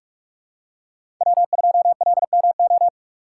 Morse code, heard as a keyed tone, about 42 wpm, 700 Hz. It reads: W2LMO
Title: USA Call Signs - 6x Character Spacing - 45wpm